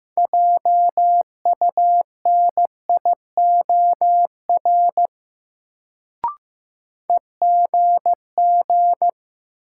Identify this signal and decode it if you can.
Morse code, heard as a keyed tone, about 15 wpm, 700 Hz. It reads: JUNIOR EGG
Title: Sentences from Top 1000 Words - Review Part 7 - New Word Drill - 15wpm